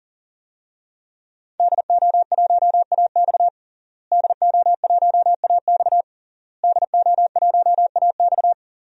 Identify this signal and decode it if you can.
Morse code, heard as a keyed tone, about 37 wpm, 700 Hz. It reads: DO1AX DO1AX DO1AX
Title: DXCC-100 Call Signs - 5 Characters - Repeated 3x - 40wpm